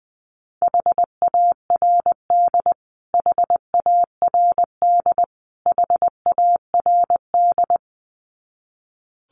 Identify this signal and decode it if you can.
Morse code, heard as a keyed tone, about 20 wpm, 700 Hz. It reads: HARD HARD HARD
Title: Top 500 Words Repeated 20wpm